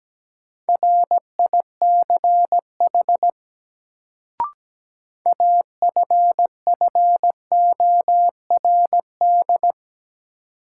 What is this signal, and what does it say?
Morse code, heard as a keyed tone, about 17 wpm, 700 Hz. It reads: RICH AFFORD
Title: Sentences from Top 1100 Words - New Word Drill Alt - 17wpm